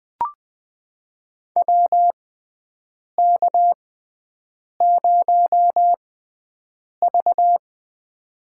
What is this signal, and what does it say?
Morse code, heard as a keyed tone, about 20 wpm, spaced as for 7 wpm, 700 Hz. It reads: WK0V
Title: USA Call Signs - 4 Characters - 6x Character Spacing - 20wpm